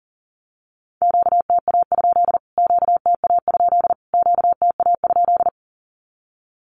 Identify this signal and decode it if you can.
Morse code, heard as a keyed tone, about 40 wpm, 700 Hz. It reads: QTA? QTA? QTA?
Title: Q-Codes Repeated 3x 40wpm